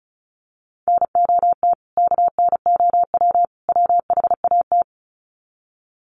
Morse code, heard as a keyed tone, about 34 wpm, 700 Hz. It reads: NOT KNOW WHAT
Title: Sets of 3 Words 35wpm